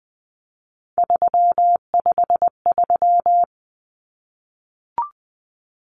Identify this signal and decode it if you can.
Morse code, heard as a keyed tone, about 20 wpm, 700 Hz. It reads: 353
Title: Numbers - 3-Digit 20wpm